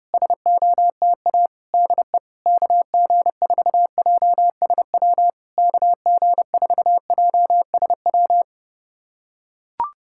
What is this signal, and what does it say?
Morse code, heard as a keyed tone, about 30 wpm, 700 Hz. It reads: SOTA DE KG4JSW KG4JSW
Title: SOTA QSO Practice 30wpm